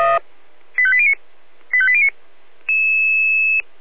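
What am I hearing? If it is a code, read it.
Five-tone selective call (ZVEI), heard as quick runs of five tones, tones of about 70 ms. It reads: 85809 85809
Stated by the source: ZVEI five-tone selective call received by SDR.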